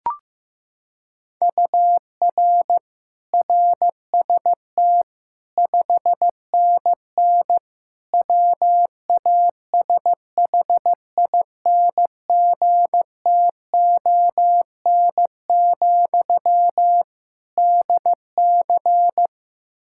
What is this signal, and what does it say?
Morse code, heard as a keyed tone, about 15 wpm, 700 Hz. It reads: UR RST 5NN WASHINGTON, DC
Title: Practice QSO - Exchange 1 - Beginning - 15wpm